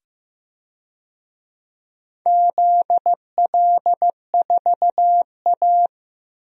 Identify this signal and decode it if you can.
Morse code, heard as a keyed tone, about 15 wpm, 700 Hz. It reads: ZL4A
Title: DXCC-100 Call Signs - 4 Characters - 15wpm